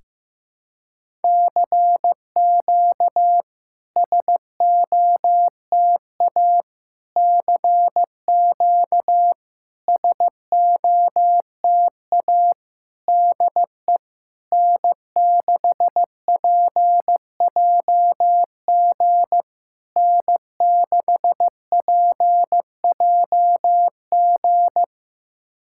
Morse code, heard as a keyed tone, about 15 wpm, 690 Hz. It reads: CQ SOTA CQ SOTA DE N6PJG N6PJG
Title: SOTA QSO Practice 15wpm